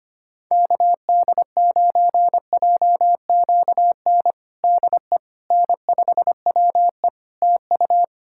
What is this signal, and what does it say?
Morse code, heard as a keyed tone, about 25 wpm, 700 Hz. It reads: KD9JQN DE N5WE TU
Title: Practice QSO - Exchange 4 - Ending - 25wpm